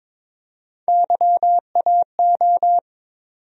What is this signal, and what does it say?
Morse code, heard as a keyed tone, about 21 wpm, 700 Hz. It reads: YAO
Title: USA Call Sign Postfixes - 3 Letters 22wpm